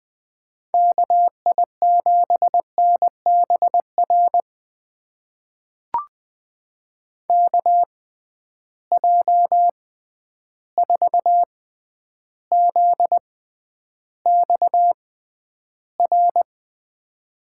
Morse code, heard as a keyed tone, about 21 wpm, spaced as for 10 wpm, 700 Hz. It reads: KI7NBR KJ4ZXR
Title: USA Call Signs - 6x Character Spacing - 20wpm